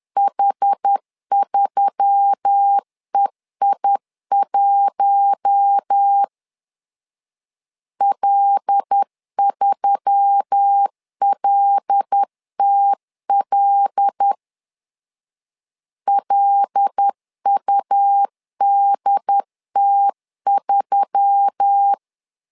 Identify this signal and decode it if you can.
Morse code, heard as a keyed tone, about 10 wpm, 800 Hz. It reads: H3EI1 L3LTL LUDT3
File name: LEZIONI / 5 LBVC / 5 L 10wpm 2259